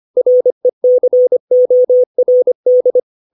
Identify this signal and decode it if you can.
Morse code, heard as a keyed tone, about 25 wpm, 500 Hz. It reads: RECORD
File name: 632 words / 632 Words 25WPM